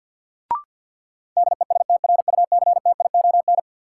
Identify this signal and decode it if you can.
Morse code, heard as a keyed tone, about 53 wpm, 700 Hz. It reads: DESTRUCTION
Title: Sets of 5 Words Encore - New Word Drill - 50wpm